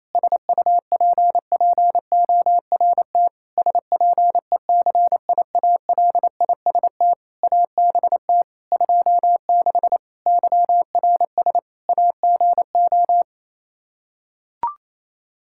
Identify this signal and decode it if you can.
Morse code, heard as a keyed tone, about 28 wpm, 700 Hz. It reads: SUPPORT SPECIALIST ABT 26 YRS AGO
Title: Practice QSO - Exchange 3 - More Info - 28wpm